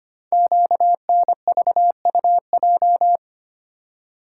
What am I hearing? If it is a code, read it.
Morse code, heard as a keyed tone, about 25 wpm, 700 Hz. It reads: QNVUJ
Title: Code Groups - 5 Letters - 25wpm